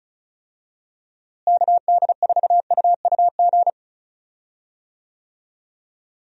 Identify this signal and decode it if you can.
Morse code, heard as a keyed tone, about 36 wpm, 700 Hz. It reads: KD4UUG
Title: USA Call Signs - 6 Characters - Repeated 2x - 35wpm